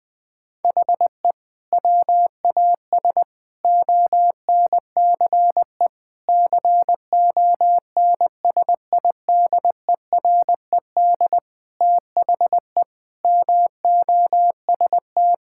Morse code, heard as a keyed tone, about 20 wpm, 700 Hz. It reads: HE WAS ONCE CONSIDERED THE MOST
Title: Sentences with 7-Word Phrases - 20wpm